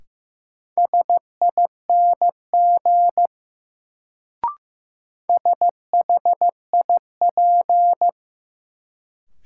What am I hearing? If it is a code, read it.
Morse code, heard as a keyed tone, about 15 wpm, 700 Hz. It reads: SING SHIP
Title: Sentences from Top 400 Words - New Word Drill - 15wpm